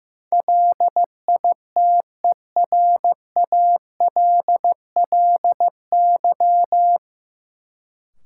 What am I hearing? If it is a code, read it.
Morse code, heard as a keyed tone, about 15 wpm, 700 Hz. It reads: LITERALLY
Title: Sentences from Top 1400 Words - New Word Drill Alt - 15wpm